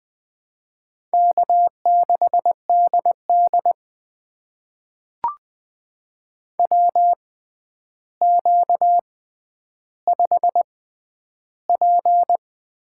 Morse code, heard as a keyed tone, about 20 wpm, spaced as for 10 wpm, 700 Hz. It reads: K6DD WQ5P
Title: USA Call Signs - 4 Characters - 6x Character Spacing - 20wpm